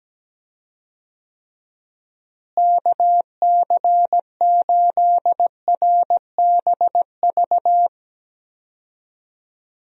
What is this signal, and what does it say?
Morse code, heard as a keyed tone, about 17 wpm, 700 Hz. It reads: KC8RBV
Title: USA Call Signs - 6 Characters - Repeated 2x - 17wpm